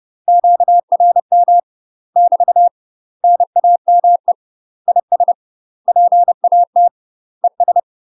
Morse code, heard as a keyed tone, about 30 wpm, 700 Hz. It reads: QRM <BT> NAME IS PAT ES
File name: UK MASTER QSOs TRAINING/30WPM